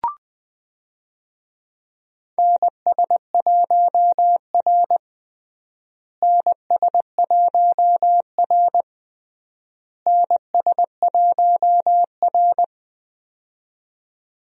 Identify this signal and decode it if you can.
Morse code, heard as a keyed tone, about 21 wpm, 700 Hz. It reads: NS1R NS1R NS1R
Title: USA Call Signs - 4 Characters - Repeated 3x - 20wpm